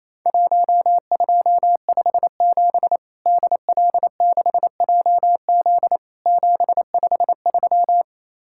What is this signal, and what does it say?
Morse code, heard as a keyed tone, about 27 wpm, 700 Hz. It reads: 1257 DL6JZ 753
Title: Worked All Europe DX Contest - QTC Exchange - 28wpm